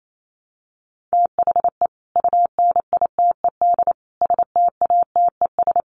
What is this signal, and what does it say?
Morse code, heard as a keyed tone, about 28 wpm, 700 Hz. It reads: THE UNITED STATES
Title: Sets of 5 Words Encore 28wpm